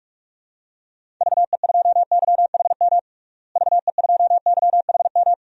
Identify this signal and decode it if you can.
Morse code, heard as a keyed tone, about 46 wpm, 690 Hz. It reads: VE2YHM VE2YHM
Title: DXCC-100 Call Signs - 6 Characters - Repeated 2x - 45wpm